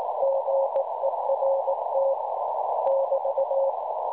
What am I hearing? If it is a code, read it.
Morse code, heard as a keyed tone, about 18 wpm, 550 Hz, with noise. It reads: GERT <BT>
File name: cwronde / websdr recording .6kHz